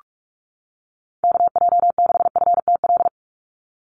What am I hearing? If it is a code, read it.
Morse code, heard as a keyed tone, about 46 wpm, 690 Hz. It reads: KJ6FTL
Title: USA Call Signs 45wpm